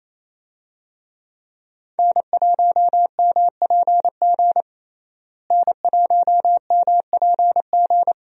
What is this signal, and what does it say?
Morse code, heard as a keyed tone, about 28 wpm, 700 Hz. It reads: N1MPG N1MPG
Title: USA Call Signs - 5 Characters - Repeated 2x - 28wpm